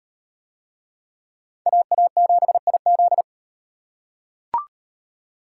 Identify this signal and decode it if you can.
Morse code, heard as a keyed tone, about 35 wpm, 700 Hz. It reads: AA7IZ